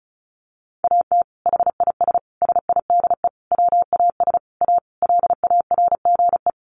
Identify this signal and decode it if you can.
Morse code, heard as a keyed tone, about 34 wpm, 700 Hz. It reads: AT HIS SIDE WAS A LARGE
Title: Sentences from Top 500 Words 35wpm